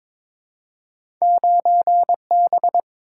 Morse code, heard as a keyed tone, about 22 wpm, 700 Hz. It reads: 9B
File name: practice / 22wpm / Two Letter-Number Combinations - 22wpm